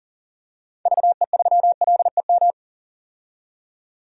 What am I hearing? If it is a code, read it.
Morse code, heard as a keyed tone, about 40 wpm, 700 Hz. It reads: VE3LEM
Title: DXCC-100 Call Signs - 6 Characters - 40wpm